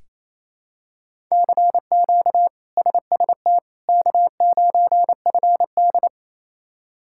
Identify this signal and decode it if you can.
Morse code, heard as a keyed tone, about 27 wpm, 700 Hz. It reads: CQ SST K9FD
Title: K1USN SST QSO Practice - 28wpm